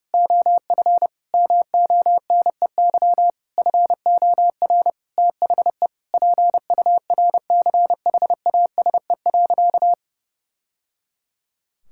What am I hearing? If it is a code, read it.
Morse code, heard as a keyed tone, about 29 wpm, 700 Hz. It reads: OF MONEY FOR THE PURCHASE.
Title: Sentences from Top 1000 Words - Review Part 5 - 30wpm